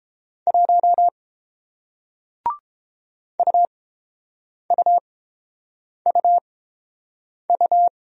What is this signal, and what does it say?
Morse code, heard as a keyed tone, about 30 wpm, spaced as for 7 wpm, 700 Hz. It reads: 1 UUUU